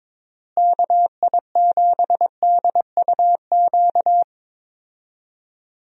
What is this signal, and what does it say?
Morse code, heard as a keyed tone, about 22 wpm, 700 Hz. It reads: KI7DUQ
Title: USA Call Signs - 6 Characters - Repeated 3x - 22wpm